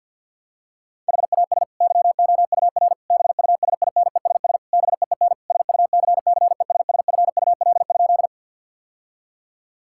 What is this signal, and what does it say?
Morse code, heard as a keyed tone, about 50 wpm, 700 Hz. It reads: HAS YOUR BUSINESS BEEN SUCCESSFUL?
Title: Sentences from Top 1000 Words - Review Part 6 - 50wpm